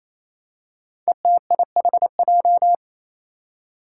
Morse code, heard as a keyed tone, about 27 wpm, 700 Hz. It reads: ETIHJ
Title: Five Challenging Letter-Number Combinations 28wpm